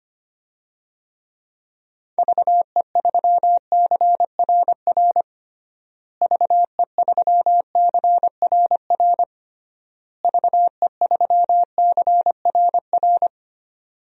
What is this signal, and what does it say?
Morse code, heard as a keyed tone, about 25 wpm, 700 Hz. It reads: VE3CRR VE3CRR VE3CRR
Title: DXCC-100 Call Signs - 6 Characters - Repeated 3x - 25wpm